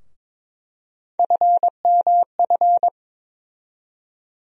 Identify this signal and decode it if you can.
Morse code, heard as a keyed tone, about 22 wpm, 700 Hz. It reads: FMF
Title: USA Call Sign Postfixes - 3 Letters 22wpm